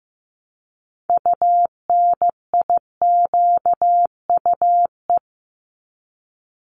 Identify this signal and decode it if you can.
Morse code, heard as a keyed tone, about 15 wpm, 700 Hz. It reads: UNIQUE
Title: Multi-letter Phonemes 15wpm